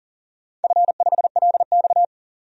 Morse code, heard as a keyed tone, about 40 wpm, 700 Hz. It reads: F5LX
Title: DXCC-100 Call Signs - 4 Characters - 40wpm